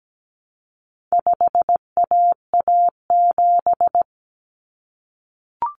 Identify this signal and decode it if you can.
Morse code, heard as a keyed tone, about 17 wpm, 700 Hz. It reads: 5AA7
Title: Common Cut Numbers - 4-Digit 17wpm